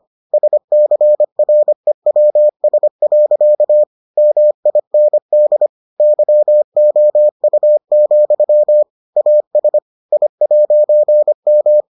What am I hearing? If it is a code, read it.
Morse code, heard as a keyed tone, about 25 wpm, 600 Hz. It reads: SCREWS. MIND YOU, AS I'M